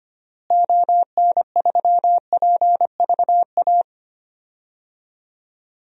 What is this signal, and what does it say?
Morse code, heard as a keyed tone, about 25 wpm, 700 Hz. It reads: ON3PVA
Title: DXCC-100 Call Signs - 6 Characters - Repeated 2x - 25wpm